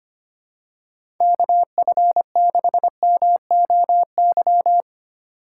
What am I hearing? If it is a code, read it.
Morse code, heard as a keyed tone, about 25 wpm, 700 Hz. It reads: KF6MOY
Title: USA Call Signs - 6 Characters - 4x Character Spacing - 25wpm